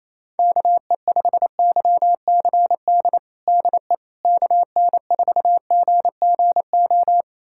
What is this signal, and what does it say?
Morse code, heard as a keyed tone, about 27 wpm, 700 Hz. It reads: KE5YCD DE KN4GGO
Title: USA Callsign DE Callsign - 6 Characters - 28wpm